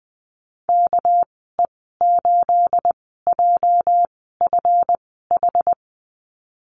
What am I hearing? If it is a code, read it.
Morse code, heard as a keyed tone, about 20 wpm, 700 Hz. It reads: K E 8 J F H
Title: 238 Lesson 38g - Letter q - Call Signs 20wpm